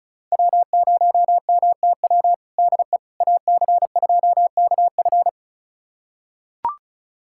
Morse code, heard as a keyed tone, about 34 wpm, 700 Hz. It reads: W0MTW DE AC2KF
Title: USA Callsign DE Callsign - 5 Characters - 35wpm